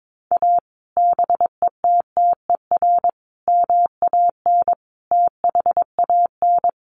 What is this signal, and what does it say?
Morse code, heard as a keyed tone, about 22 wpm, 700 Hz. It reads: A BETTER MAN THAN
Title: Sentences from Top 400 Words 22wpm